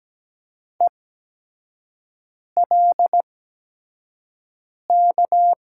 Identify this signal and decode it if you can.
Morse code, heard as a keyed tone, about 17 wpm, 700 Hz. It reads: E L K
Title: USA Call Signs - 6 Characters - 8x Character Spacing - 17wpm